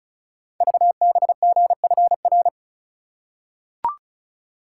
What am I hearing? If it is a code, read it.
Morse code, heard as a keyed tone, about 36 wpm, 690 Hz. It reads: VBGFR
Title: Code Groups - 5 Letters - 35wpm